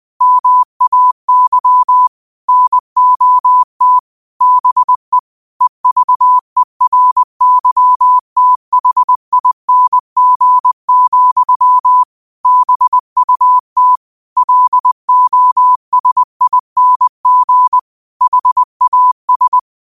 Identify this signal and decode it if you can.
Morse code, heard as a keyed tone, about 20 wpm, 1000 Hz. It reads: MAY NOT BE EVERYTHING, BUT LOSING HAS
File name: qotdfiles / 20wpm